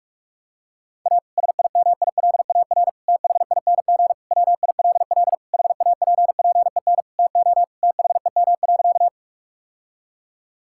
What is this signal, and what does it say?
Morse code, heard as a keyed tone, about 41 wpm, 700 Hz. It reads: A SIMILAR THING WILL HAPPEN TO THEM.
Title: Sentences from Top 1000 Words - Review Part 5 - 45wpm